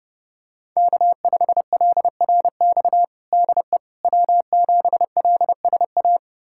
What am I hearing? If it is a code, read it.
Morse code, heard as a keyed tone, about 32 wpm, 700 Hz. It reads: K5LRX DE W7LSA
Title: USA Callsign DE Callsign 30wpm